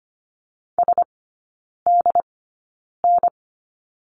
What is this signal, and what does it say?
Morse code, heard as a keyed tone, about 25 wpm, spaced as for 9 wpm, 690 Hz. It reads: SDN
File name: Lesson 14e - Number 2 - Three Characters 25wpm